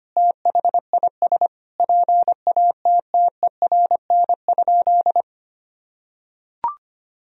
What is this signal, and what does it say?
Morse code, heard as a keyed tone, about 25 wpm, 700 Hz. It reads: THIS PATTERN?
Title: Sentences from Top 1000 Words - Review Part 2 - 25wpm